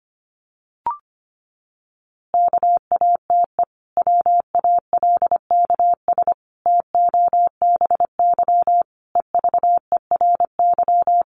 Morse code, heard as a keyed tone, about 25 wpm, 700 Hz. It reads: KATE WALKS TOBY EVERY
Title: Easy Phrases 25wpm